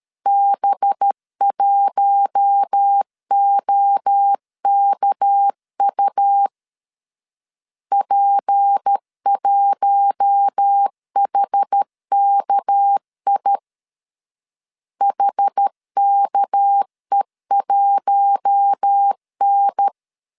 Morse code, heard as a keyed tone, about 13 wpm, 800 Hz. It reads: B1OKU P1HKI HKE1N
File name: LEZIONI / 6 PKX0 / 6 PK 12wpm 2006